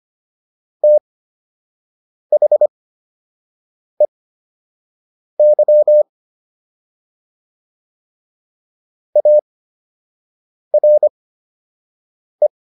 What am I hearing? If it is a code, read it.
Morse code, heard as a keyed tone, about 25 wpm, spaced as for 6 wpm, 600 Hz. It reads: THEY ARE